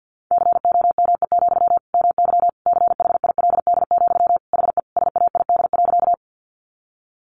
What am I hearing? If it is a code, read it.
Morse code, heard as a keyed tone, about 50 wpm, 700 Hz. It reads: COME, MY CHILD, HE SAID.
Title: Sentences from Top 800 Words 50wpm